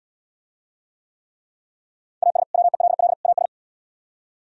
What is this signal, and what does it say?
Morse code, heard as a keyed tone, about 76 wpm, 700 Hz. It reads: AS WELL AS